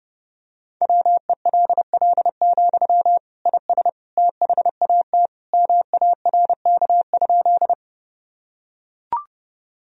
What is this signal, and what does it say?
Morse code, heard as a keyed tone, about 30 wpm, 700 Hz. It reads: WELL, IS THAT MARK?
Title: Sentences from Top 1000 Words - Review Part 1 - 30wpm